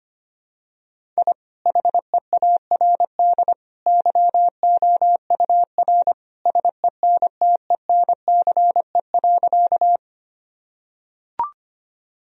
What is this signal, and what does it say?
Morse code, heard as a keyed tone, about 25 wpm, 700 Hz. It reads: I HEARD YOUR SENTENCE.
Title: Sentences from Top 1000 Words - Review Part 1 - 25wpm